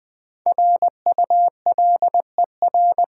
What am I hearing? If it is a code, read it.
Morse code, heard as a keyed tone, about 21 wpm, 700 Hz. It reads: RULER
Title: Easy Words with Hints - 20wpm